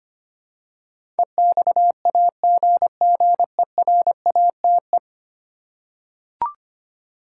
Morse code, heard as a keyed tone, about 25 wpm, 700 Hz. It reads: EXAGGERATE
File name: Ten Letter Words - 25wpm